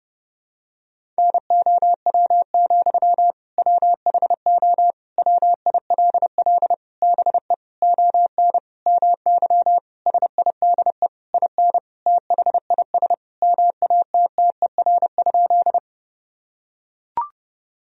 Morse code, heard as a keyed tone, about 31 wpm, 700 Hz. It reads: NOW, WHO WILL BE ON MY SIDE IN THIS MATTER?
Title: Sentences from Top 1000 Words - Review Part 1 - 30wpm